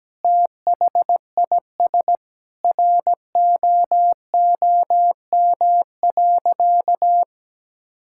Morse code, heard as a keyed tone, about 17 wpm, 700 Hz. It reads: THIS ROOM.
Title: Sentences from Top 1000 Words - Review Part 6 - 17wpm